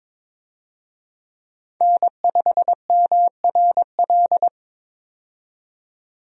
Morse code, heard as a keyed tone, about 22 wpm, 700 Hz. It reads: N5MRL
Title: USA Call Signs - 5 Characters - 22wpm